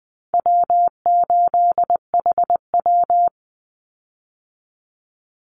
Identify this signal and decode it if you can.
Morse code, heard as a keyed tone, about 20 wpm, 700 Hz. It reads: W8HW
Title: Easy Call Signs Repeated 3x - 20wpm